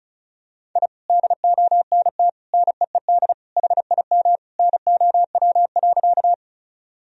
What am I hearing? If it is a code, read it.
Morse code, heard as a keyed tone, about 34 wpm, 700 Hz. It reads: I DONT NEED HIM NOW.
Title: Sentences from Top 1000 Words - Review Part 1 - 35wpm